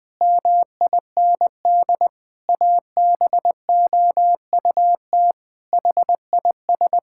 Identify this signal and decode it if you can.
Morse code, heard as a keyed tone, about 20 wpm, 700 Hz. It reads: MIND ABOUT HIS
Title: Sentences with 7-Word Phrases - Spoken Beginning - 20wpm